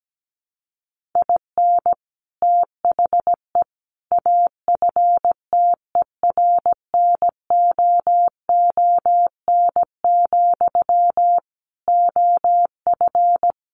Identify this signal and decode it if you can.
Morse code, heard as a keyed tone, about 17 wpm, 700 Hz. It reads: IN THE AFTERNOON, OF
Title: Sentences from Top 1500 Words 17wpm